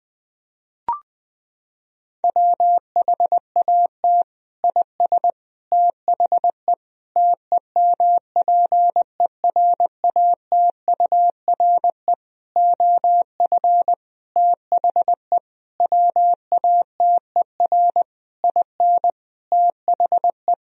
Morse code, heard as a keyed tone, about 20 wpm, 700 Hz. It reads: WHAT IS THE TEMPERATURE OF THE WATER IN THE
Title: Sentences from Top 1000 Words - Review Part 4 - 20wpm